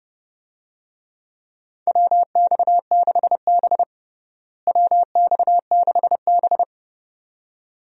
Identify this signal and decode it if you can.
Morse code, heard as a keyed tone, about 30 wpm, 700 Hz. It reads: WX6B WX6B
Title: USA Call Signs - 4 Characters - Repeated 2x - 30wpm